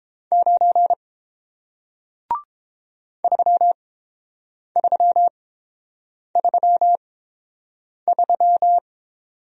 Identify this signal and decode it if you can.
Morse code, heard as a keyed tone, about 30 wpm, spaced as for 7 wpm, 700 Hz. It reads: 9 3333